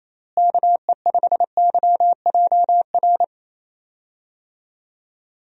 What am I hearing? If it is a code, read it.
Morse code, heard as a keyed tone, about 28 wpm, 700 Hz. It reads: KE5YJR
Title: USA Call Signs - 6 Characters - Repeated 3x - 28wpm